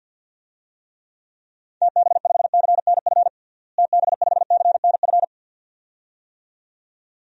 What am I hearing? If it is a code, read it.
Morse code, heard as a keyed tone, about 52 wpm, 700 Hz. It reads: TB5CNF TB5CNF
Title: DXCC-100 Call Signs - 6 Characters - Repeated 2x - 50wpm